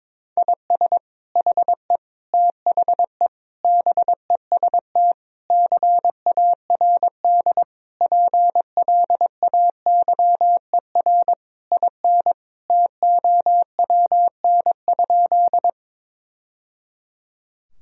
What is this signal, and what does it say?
Morse code, heard as a keyed tone, about 22 wpm, 700 Hz. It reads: IS HE THE BEST CARD PLAYER IN TOWN?
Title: Sentences from Top 1000 Words - Review Part 4 - 22wpm